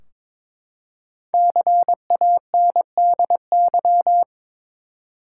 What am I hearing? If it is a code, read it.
Morse code, heard as a keyed tone, about 22 wpm, 700 Hz. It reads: CANDY
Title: Five Letter Words - 22wpm